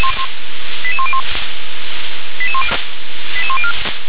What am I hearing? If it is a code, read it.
Five-tone selective call (ZVEI), heard as quick runs of five tones, tones of about 70 ms. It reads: 88101 88104
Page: /357/SDR/ZVEI/88778/